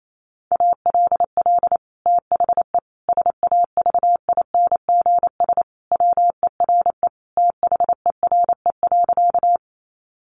Morse code, heard as a keyed tone, about 28 wpm, 700 Hz. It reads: ALL THE SAVINGS WERE THERE.
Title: Sentences from Top 1200 Words 28wpm